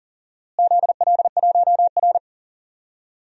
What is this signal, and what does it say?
Morse code, heard as a keyed tone, about 40 wpm, 700 Hz. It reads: ZL1R
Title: DXCC-100 Call Signs - 4 Characters - Repeated 3x - 40wpm